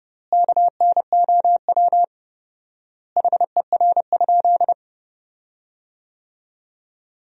Morse code, heard as a keyed tone, about 30 wpm, 700 Hz. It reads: KNOW HER?
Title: 4x Word-Spacing - Sentences from Top 100 Words - 30wpm